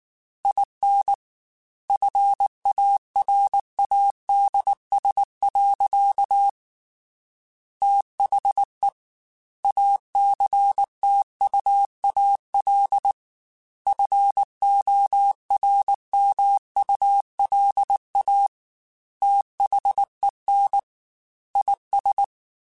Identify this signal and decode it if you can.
Morse code, heard as a keyed tone, about 19 wpm, 800 Hz. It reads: IN FARADS. THE ACTUAL FORMULA THEN IS
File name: Technical Subjects / Technical Subjects-mp3s / XC @18